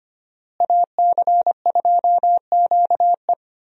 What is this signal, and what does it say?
Morse code, heard as a keyed tone, about 25 wpm, 700 Hz. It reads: AC2QE
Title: USA Call Signs - 5 Characters - Repeated 2x - 25wpm